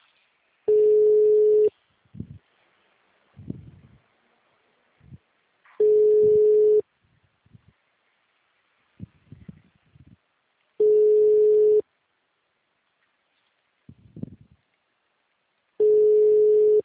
Ringback tone (Europe, 425 Hz).
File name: out- -2014- - - .61